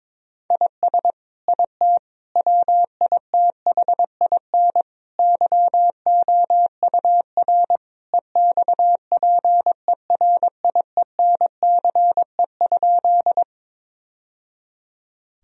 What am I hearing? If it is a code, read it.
Morse code, heard as a keyed tone, about 22 wpm, 700 Hz. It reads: IS IT WITHIN YOUR EXPERIENCE?
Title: Sentences from Top 1000 Words - Review Part 4 - 22wpm